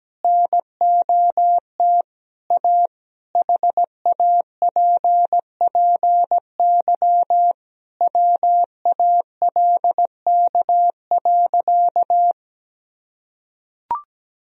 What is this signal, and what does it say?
Morse code, heard as a keyed tone, about 17 wpm, 700 Hz. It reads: NOT A HAPPY WALK.
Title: Sentences from Top 1000 Words - Review Part 1 - 17wpm